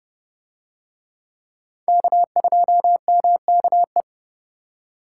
Morse code, heard as a keyed tone, about 30 wpm, 700 Hz. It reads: K2MKE